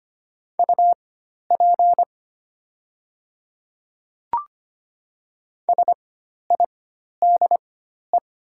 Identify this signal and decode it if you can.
Morse code, heard as a keyed tone, about 25 wpm, spaced as for 12 wpm, 700 Hz. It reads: UP SIDE
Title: 4x Character Spacing - Top 100 Words - 25wpm